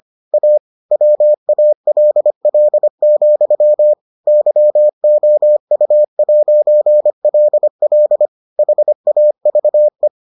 Morse code, heard as a keyed tone, about 25 wpm, 600 Hz. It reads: A WALL, YOU'LL HAVE